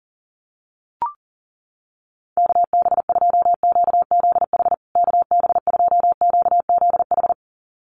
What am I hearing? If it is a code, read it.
Morse code, heard as a keyed tone, about 40 wpm, 700 Hz. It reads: KB2QZH KB2QZH
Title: USA Call Signs Repeated 2x 40wpm